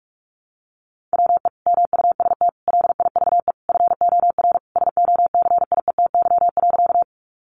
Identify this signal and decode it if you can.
Morse code, heard as a keyed tone, about 44 wpm, 700 Hz. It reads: WE MUST LIVE FOR SOCIETY.
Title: Sentences from Top 700 Words 45wpm